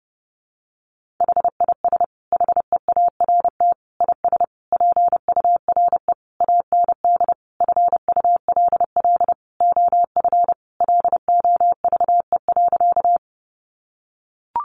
Morse code, heard as a keyed tone, about 30 wpm, 700 Hz. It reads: HIS HEART IS PURE AND FULL OF LOVE.
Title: Sentences from Top 1500 Words 30wpm